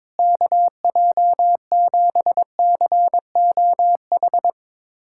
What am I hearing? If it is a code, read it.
Morse code, heard as a keyed tone, about 22 wpm, 700 Hz. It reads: KJ7COH
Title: USA Callsign DE Callsign 22wpm